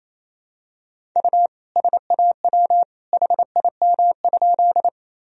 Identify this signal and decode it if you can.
Morse code, heard as a keyed tone, about 28 wpm, 700 Hz. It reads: U SAW HIM?
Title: Sentences from Top 300 Words and Abbreviations 28wpm